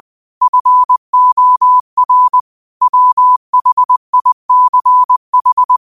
Morse code, heard as a keyed tone, about 20 wpm, 1000 Hz. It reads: FOR WHICH
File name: qotdfiles / 20wpm